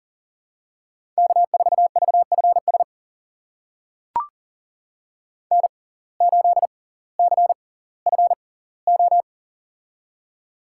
Morse code, heard as a keyed tone, about 40 wpm, spaced as for 20 wpm, 700 Hz. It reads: K4VFS N8CFO
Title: USA Call Signs - 5 Characters - 6x Character Spacing - 40wpm